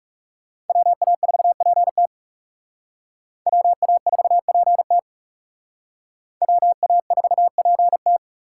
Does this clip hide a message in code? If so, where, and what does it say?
Yes: Morse code from 0.7 to 8.2 s, heard as a keyed tone, about 38 wpm, 700 Hz. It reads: WA4PT WA4PT WA4PT